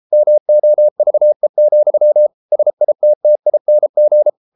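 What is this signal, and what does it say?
Morse code, heard as a keyed tone, about 33 wpm, 600 Hz. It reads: MOVE, SITTING
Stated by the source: Morse code practice at 32 wpm from CW med Gnister, reading Ken Follett's novel Never in English.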